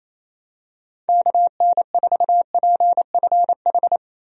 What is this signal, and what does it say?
Morse code, heard as a keyed tone, about 28 wpm, 700 Hz. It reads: KN4PFH
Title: USA Call Signs - 6 Characters - Repeated 3x - 28wpm